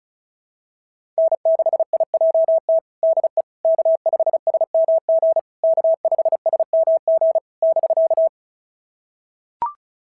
Morse code, heard as a keyed tone, about 36 wpm, 690 Hz. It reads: N6IJT DE K5SMG K5SMG <BK>
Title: SOTA QSO Practice 35wpm